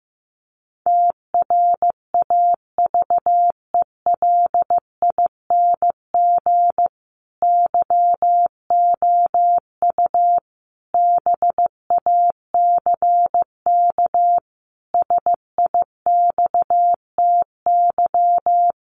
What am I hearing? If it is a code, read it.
Morse code, heard as a keyed tone, about 15 wpm, 700 Hz. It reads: TRAVELING YOU BACK SIXTY
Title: A Sound Of Thunder 15wpm